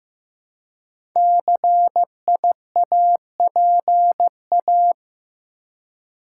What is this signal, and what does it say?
Morse code, heard as a keyed tone, about 15 wpm, 700 Hz. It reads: CIAPA
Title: Code Groups - 5 Letters - 15wpm